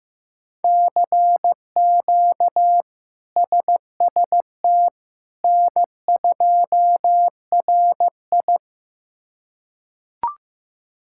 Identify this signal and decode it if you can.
Morse code, heard as a keyed tone, about 15 wpm, 700 Hz. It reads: CQ SST N2RI
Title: K1USN SST QSO Practice - 15wpm